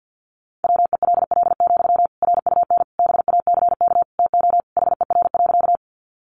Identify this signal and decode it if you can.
Morse code, heard as a keyed tone, about 50 wpm, 700 Hz. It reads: WELL, RUN BACK TO HER.
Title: Sentences from Top 300 Words 50wpm